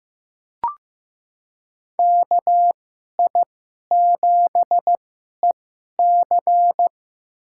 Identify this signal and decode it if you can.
Morse code, heard as a keyed tone, about 15 wpm, spaced as for 11 wpm, 700 Hz. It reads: KI7EC
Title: USA Call Signs - 2x Character Spacing - 15wpm